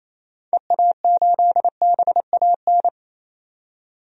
Morse code, heard as a keyed tone, about 27 wpm, 700 Hz. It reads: EA8BAN
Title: DXCC-100 Call Signs - 6 Characters - 28wpm